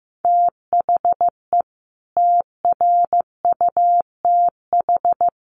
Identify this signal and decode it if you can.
Morse code, heard as a keyed tone, about 15 wpm, 700 Hz. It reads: THE TRUTH
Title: Sets of 5 Words Encore 15wpm